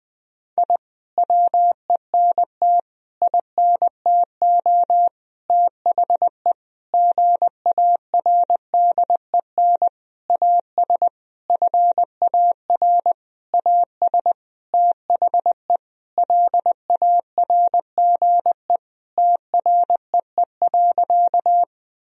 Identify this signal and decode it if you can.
Morse code, heard as a keyed tone, about 20 wpm, 700 Hz. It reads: I WENT INTO THE GARDEN AS FAR AS THE LARGE TREE.
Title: Sentences from Top 1000 Words - Review Part 4 - 20wpm